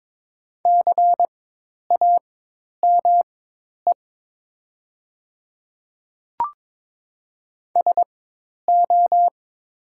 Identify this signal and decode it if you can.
Morse code, heard as a keyed tone, about 22 wpm, spaced as for 10 wpm, 700 Hz. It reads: CAME SO
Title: 4x Character Spacing - Top 200 Words - 22wpm